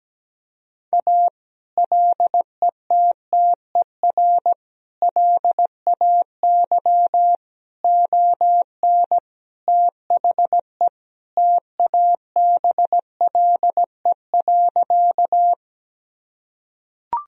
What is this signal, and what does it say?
Morse code, heard as a keyed tone, about 17 wpm, 700 Hz. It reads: A LETTER LAY ON THE TABLE.
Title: Sentences from Top 1000 Words - Review Part 2 - 17wpm